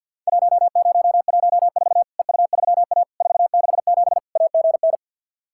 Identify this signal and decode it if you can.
Morse code, heard as a keyed tone, about 49 wpm, 700 Hz. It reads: 1014 EU3A 467 AGN
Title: Worked All Europe DX Contest - QTC Exchange with Repeats - 50wpm